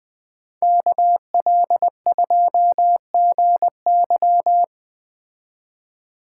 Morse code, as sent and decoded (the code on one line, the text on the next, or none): -.- .-.. ..--- --. -.--
KL2GY